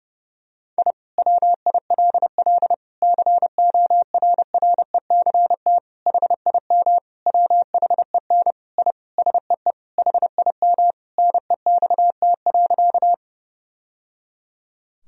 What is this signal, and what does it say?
Morse code, heard as a keyed tone, about 30 wpm, 700 Hz. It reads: I WILL CORRECT HIM WHEN I SEE HIM NEXT.
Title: Sentences from Top 1000 Words - Review Part 2 - 30wpm